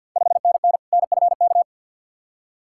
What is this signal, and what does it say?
Morse code, heard as a keyed tone, about 50 wpm, 700 Hz. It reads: 5NN NFK
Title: Ontario QSO Party Exchange 50wpm